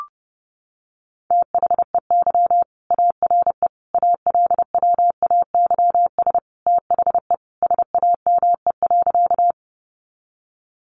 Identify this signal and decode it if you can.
Morse code, heard as a keyed tone, about 30 wpm, 700 Hz. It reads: THEY ARE ALWAYS THE SAME.
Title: Sentences from Top 300 Words 30wpm